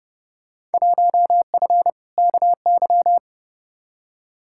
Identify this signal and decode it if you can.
Morse code, heard as a keyed tone, about 31 wpm, 700 Hz. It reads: 1F KY
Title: Field Day Exchanges Encore - 30wpm